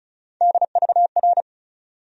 Morse code, heard as a keyed tone, about 36 wpm, 700 Hz. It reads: DVR
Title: USA Call Sign Postfixes - 3 Letters 35wpm